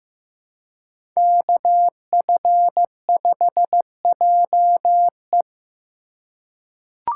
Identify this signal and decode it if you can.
Morse code, heard as a keyed tone, about 15 wpm, 700 Hz. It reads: KF5JE
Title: USA Call Signs - 5 Characters - 2x Character Spacing - 15wpm